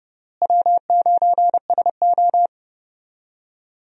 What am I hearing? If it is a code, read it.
Morse code, heard as a keyed tone, about 29 wpm, 700 Hz. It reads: W9SO
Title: USA Call Signs - 4 Characters - Repeated 3x - 30wpm